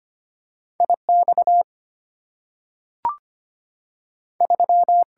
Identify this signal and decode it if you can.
Morse code, heard as a keyed tone, about 25 wpm, 690 Hz. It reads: IX 3
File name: Two Letter-Number Combinations - 25wpm